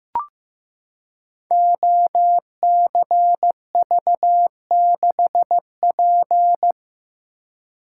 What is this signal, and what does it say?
Morse code, heard as a keyed tone, about 15 wpm, 700 Hz. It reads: OCV6P
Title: Code Groups - 5 Letter:Numbers - 15wpm